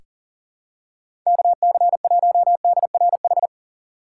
Morse code, heard as a keyed tone, about 40 wpm, 700 Hz. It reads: KC1DRH
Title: USA Call Signs - 6x Character Spacing - 40wpm